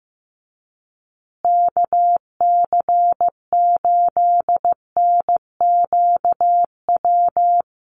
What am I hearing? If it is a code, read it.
Morse code, heard as a keyed tone, about 15 wpm, 700 Hz. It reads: KC8NQW
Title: USA Call Signs Repeated 3x 15wpm